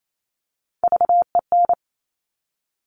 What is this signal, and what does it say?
Morse code, heard as a keyed tone, about 28 wpm, 700 Hz. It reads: VEN
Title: Top Three Letter Combinations in Words 28wpm